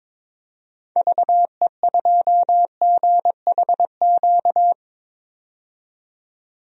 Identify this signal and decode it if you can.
Morse code, heard as a keyed tone, about 22 wpm, 700 Hz. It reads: VE2GHQ
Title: DXCC-100 Call Signs - 6 Characters - 22wpm